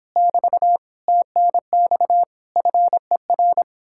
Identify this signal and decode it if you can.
Morse code, heard as a keyed tone, about 29 wpm, 700 Hz. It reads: <BT> TNX FER